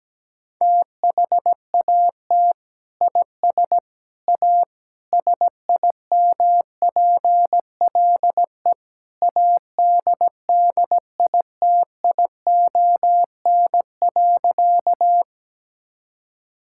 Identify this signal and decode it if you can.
Morse code, heard as a keyed tone, about 17 wpm, 700 Hz. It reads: THAT IS A SIMPLE ADDITION.
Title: Sentences from Top 1000 Words - Review Part 6 - 17wpm